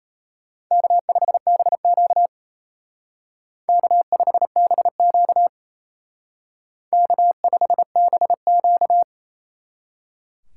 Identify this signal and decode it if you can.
Morse code, heard as a keyed tone, about 34 wpm, 700 Hz. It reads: K5BQ K5BQ K5BQ
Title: Speed Racing - USA Call Signs 28wpm